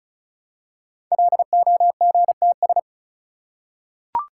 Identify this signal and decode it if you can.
Morse code, heard as a keyed tone, about 36 wpm, 700 Hz. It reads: LOGTS
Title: Code Groups - 5 Letter:Numbers - 35wpm